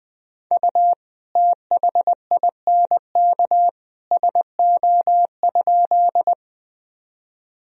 Morse code, heard as a keyed tone, about 19 wpm, 700 Hz. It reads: U THINK SO?
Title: Sentences from Top 200 Words and Abbreviations 20wpm